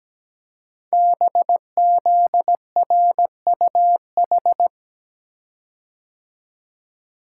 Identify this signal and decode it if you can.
Morse code, heard as a keyed tone, about 17 wpm, 700 Hz. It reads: BZRUH
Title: Code Groups - 5 Letters - 17wpm